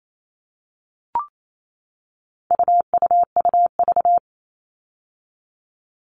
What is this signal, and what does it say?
Morse code, heard as a keyed tone, about 28 wpm, 700 Hz. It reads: UUUV
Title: All Cut Numbers - 4-Digit 28wpm